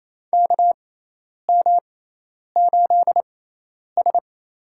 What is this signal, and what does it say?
Morse code, heard as a keyed tone, about 28 wpm, spaced as for 10 wpm, 700 Hz. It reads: KM8S
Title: USA Call Signs - 4 Characters - 6x Character Spacing - 28wpm